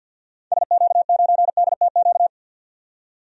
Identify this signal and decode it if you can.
Morse code, heard as a keyed tone, about 47 wpm, 700 Hz. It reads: SQ9DTQ
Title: DXCC-100 Call Signs - 6 Characters - 50wpm